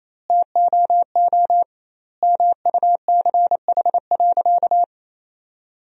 Morse code, heard as a keyed tone, about 28 wpm, 700 Hz. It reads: TOO MUCH.
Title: 2x Word-Spacing - Sentences from Top 400 Words - 28wpm